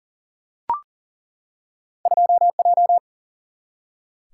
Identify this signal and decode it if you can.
Morse code, heard as a keyed tone, about 40 wpm, 700 Hz. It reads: 2J
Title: Single-Letter Single-Number 40wpm